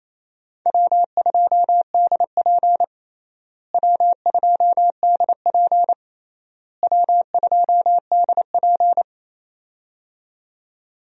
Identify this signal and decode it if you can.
Morse code, heard as a keyed tone, about 28 wpm, 700 Hz. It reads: W2DP W2DP W2DP
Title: USA Call Signs - 4 Characters - Repeated 3x - 28wpm